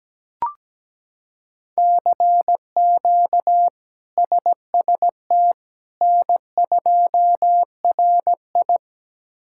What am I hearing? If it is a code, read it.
Morse code, heard as a keyed tone, about 17 wpm, 700 Hz. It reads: CQ SST N2RI
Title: K1USN SST QSO Practice - 17wpm